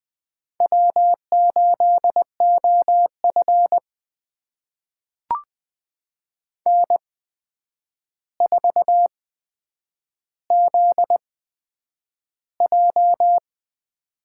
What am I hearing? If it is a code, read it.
Morse code, heard as a keyed tone, about 20 wpm, spaced as for 9 wpm, 700 Hz. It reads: W8OF N4ZJ
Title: USA Call Signs - 4 Characters - 8x Character Spacing - 20wpm